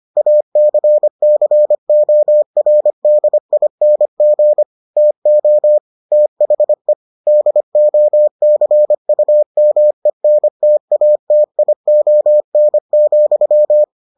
Morse code, heard as a keyed tone, about 25 wpm, 600 Hz. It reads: ACCORDING TO THE DOCUMENTATION,